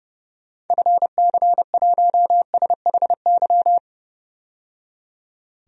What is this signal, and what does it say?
Morse code, heard as a keyed tone, about 32 wpm, 700 Hz. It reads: FC1SHY
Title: DXCC-100 Call Signs - 6 Characters - 30wpm